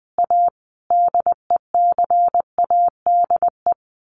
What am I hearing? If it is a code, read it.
Morse code, heard as a keyed tone, about 20 wpm, 700 Hz. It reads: A DECADE
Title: Sets of 5 Words 20wpm